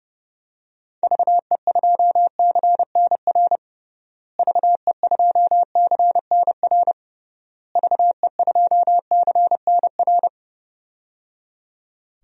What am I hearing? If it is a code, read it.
Morse code, heard as a keyed tone, about 30 wpm, 700 Hz. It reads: VE2CNR VE2CNR VE2CNR
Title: DXCC-100 Call Signs - 6 Characters - Repeated 3x - 30wpm